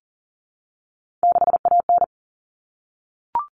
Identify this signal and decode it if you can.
Morse code, heard as a keyed tone, about 40 wpm, 700 Hz. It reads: 6AN
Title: Common Cut Numbers - 3-Digit 40wpm